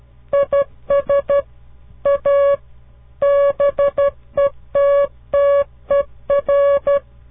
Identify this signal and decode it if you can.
Morse code, heard as a keyed tone, about 13 wpm, 560 Hz. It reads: IS A BETTER
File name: CW Learning Programs / farnsworth / unit 2